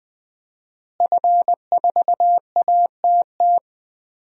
Morse code, heard as a keyed tone, about 20 wpm, 700 Hz. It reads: F4ATT
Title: DXCC-100 Call Signs - 5 Characters - Repeated 3x - 20wpm